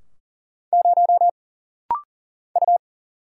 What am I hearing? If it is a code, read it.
Morse code, heard as a keyed tone, about 40 wpm, 700 Hz. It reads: 0 U
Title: Single Letter-Number - Rapid-Fire 40wpm